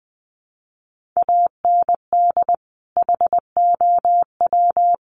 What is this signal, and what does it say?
Morse code, heard as a keyed tone, about 20 wpm, 700 Hz. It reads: AND HOW
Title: Sentences from Top 1500 Words 20wpm